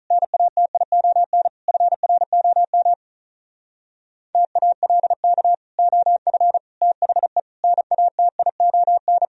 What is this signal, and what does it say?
Morse code, heard as a keyed tone, about 44 wpm, 700 Hz. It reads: NATION FROM TALK OF THE NATION
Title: Speed-Racing - Sets of 5 Words 35wpm